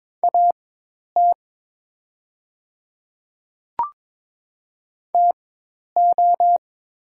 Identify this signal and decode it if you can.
Morse code, heard as a keyed tone, about 23 wpm, spaced as for 10 wpm, 690 Hz. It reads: AT TO
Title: 4x Character Spacing - Top 200 Words - 22wpm